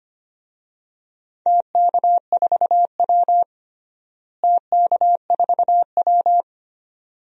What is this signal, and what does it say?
Morse code, heard as a keyed tone, about 25 wpm, 700 Hz. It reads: TK4W TK4W
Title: DXCC-100 Call Signs - 4 Characters - Repeated 2x - 25wpm